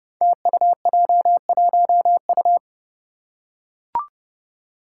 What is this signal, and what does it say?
Morse code, heard as a keyed tone, about 30 wpm, 700 Hz. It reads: TUJ1U
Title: Five Challenging Letter-Number Combinations 30wpm